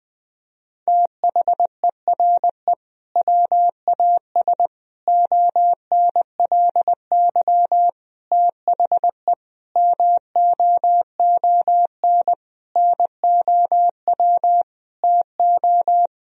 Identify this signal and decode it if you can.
Morse code, heard as a keyed tone, about 20 wpm, 700 Hz. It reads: THERE WAS ONLY THE MOON NOW TO
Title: Sentences from Top 1000 Words - Review Part 3 - 20wpm